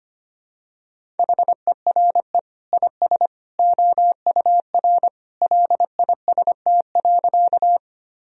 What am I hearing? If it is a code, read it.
Morse code, heard as a keyed tone, about 25 wpm, 700 Hz. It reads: HERE IS OUR LIST.
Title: Sentences from Top 1000 Words - Review Part 2 - 25wpm